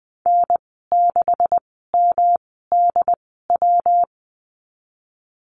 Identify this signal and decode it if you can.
Morse code, heard as a keyed tone, about 20 wpm, spaced as for 15 wpm, 700 Hz. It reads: N6MDW
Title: 167 Lesson 27g - Number 7 - Call Signs 20wpm